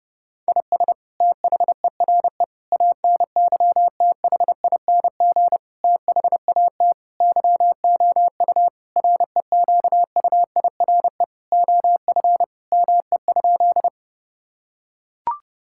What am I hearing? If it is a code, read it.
Morse code, heard as a keyed tone, about 32 wpm, 700 Hz. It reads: IS THERE ANYTHING THAT YOU REQUIRE OF ME?
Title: Sentences from Top 1000 Words - Review Part 6 - 30wpm